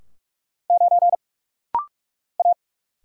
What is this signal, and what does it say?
Morse code, heard as a keyed tone, about 44 wpm, 700 Hz. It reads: 9 A
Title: Single Letter-Number - Rapid-Fire 45wpm